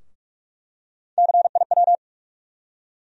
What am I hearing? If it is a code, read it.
Morse code, heard as a keyed tone, about 46 wpm, 700 Hz. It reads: CIW
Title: USA Call Sign Postfixes - 3 Letters 45wpm